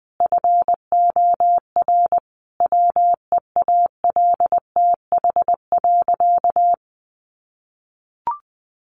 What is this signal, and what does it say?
Morse code, heard as a keyed tone, about 20 wpm, 700 Hz. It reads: FOR WEALTH.
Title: Sentences from Top 1200 Words 20wpm